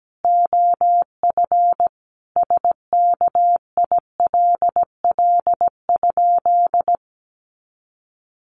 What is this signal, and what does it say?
Morse code, heard as a keyed tone, about 17 wpm, 700 Hz. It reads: OF SKILL?
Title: Sentences from Top 1200 Words 17wpm